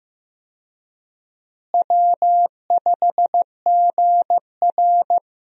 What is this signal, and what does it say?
Morse code, heard as a keyed tone, about 15 wpm, 700 Hz. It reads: W5GR
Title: USA Call Signs - 4 Characters - Repeated 3x - 15wpm